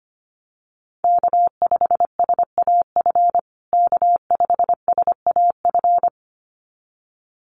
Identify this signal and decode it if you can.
Morse code, heard as a keyed tone, about 25 wpm, 700 Hz. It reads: K5SAF K5SAF
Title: USA Call Signs Repeated 2x 25wpm